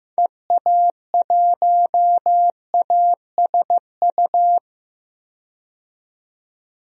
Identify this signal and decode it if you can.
Morse code, heard as a keyed tone, about 15 wpm, 700 Hz. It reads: EA1ASU
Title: DXCC-100 Call Signs - 6 Characters - 15wpm